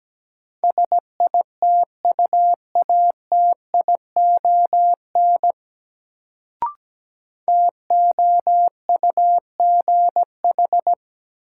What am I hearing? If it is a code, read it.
Morse code, heard as a keyed tone, about 17 wpm, 700 Hz. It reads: SITUATION TOUGH
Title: Sentences from Top 1200 Words - New Word Drill - 17wpm